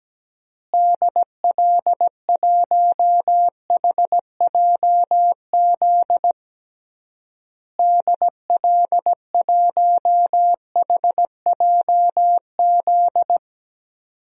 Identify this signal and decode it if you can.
Morse code, heard as a keyed tone, about 17 wpm, 700 Hz. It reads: DL1HJZ DL1HJZ
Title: DXCC-100 Call Signs - 6 Characters - Repeated 3x - 17wpm